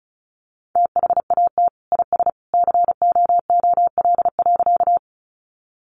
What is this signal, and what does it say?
Morse code, heard as a keyed tone, about 34 wpm, 700 Hz. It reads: THAT IS COOL.
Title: Sentences from Top 1200 Words 35wpm